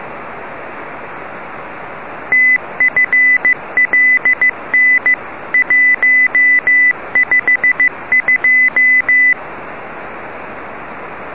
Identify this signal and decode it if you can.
Morse code, heard as a keyed tone, about 15 wpm, 2000 Hz. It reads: TFLN 152
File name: rs-21 1515utc